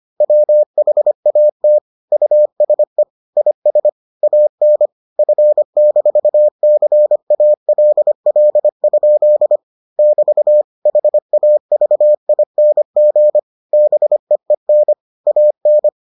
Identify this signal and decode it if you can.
Morse code, heard as a keyed tone, about 25 wpm, 600 Hz. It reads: WHAT USE IS AN F-CALL? <BT> HAVING BEEN AN